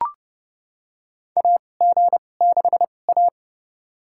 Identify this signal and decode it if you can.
Morse code, heard as a keyed tone, about 29 wpm, spaced as for 21 wpm, 700 Hz. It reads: AG6A
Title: USA Call Signs - 4 Characters - 2x Character Spacing - 30wpm